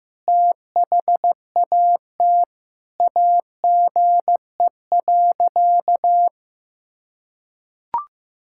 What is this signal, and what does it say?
Morse code, heard as a keyed tone, about 16 wpm, 700 Hz. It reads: THAT AGE.
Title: Sentences from Top 1000 Words - Review Part 6 - 15wpm